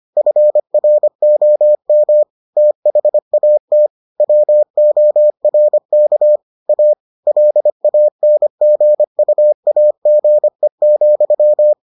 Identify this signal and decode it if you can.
Morse code, heard as a keyed tone, about 25 wpm, 600 Hz. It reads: FROM THAT WORK A LANGUAGE,